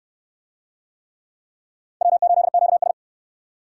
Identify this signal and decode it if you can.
Morse code, heard as a keyed tone, about 68 wpm, 700 Hz. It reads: W8QS